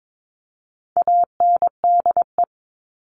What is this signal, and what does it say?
Morse code, heard as a keyed tone, about 22 wpm, 700 Hz. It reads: ANDE
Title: All Cut Numbers - 4-Digit 22wpm